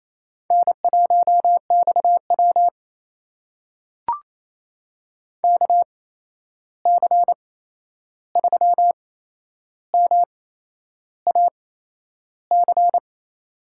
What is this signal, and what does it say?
Morse code, heard as a keyed tone, about 28 wpm, spaced as for 11 wpm, 700 Hz. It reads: N1XW KC3MAC
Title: USA Call Signs - 8x Character Spacing - 28wpm